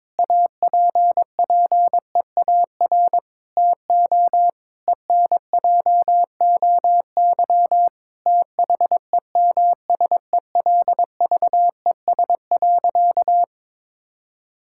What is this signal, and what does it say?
Morse code, heard as a keyed tone, about 21 wpm, 700 Hz. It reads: APPEAR TO ENJOY THEMSELVES.
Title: Sentences from Top 1000 Words - Review Part 5 - 22wpm